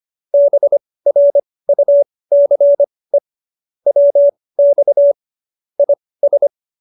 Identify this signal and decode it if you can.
Morse code, heard as a keyed tone, about 25 wpm, spaced as for 18 wpm, 570 Hz. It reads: BRUCE WX IS